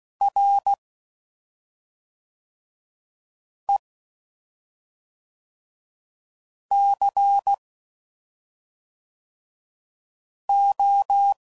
Morse code, heard as a keyed tone, about 16 wpm, 800 Hz. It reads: R E C O
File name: Suffixes / Special ING's / SlowToFaster ING's